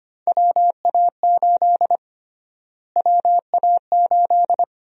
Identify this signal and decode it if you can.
Morse code, heard as a keyed tone, about 25 wpm, 700 Hz. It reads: WA8 WA8
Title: USA Call Sign Prefixes - Repeated 2x - 25wpm